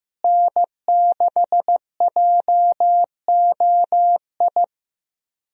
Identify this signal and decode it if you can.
Morse code, heard as a keyed tone, about 15 wpm, 700 Hz. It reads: N6JOI
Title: USA Call Signs - 5 Characters - Repeated 2x - 15wpm